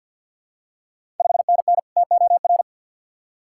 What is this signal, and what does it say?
Morse code, heard as a keyed tone, about 50 wpm, 690 Hz. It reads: 5NN TOR
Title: Ontario QSO Party Exchange 50wpm